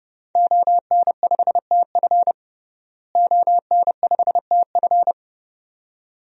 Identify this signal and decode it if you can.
Morse code, heard as a keyed tone, about 30 wpm, 700 Hz. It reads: ON5TF ON5TF
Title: DXCC-100 Call Signs - 5 Characters - Repeated 2x - 30wpm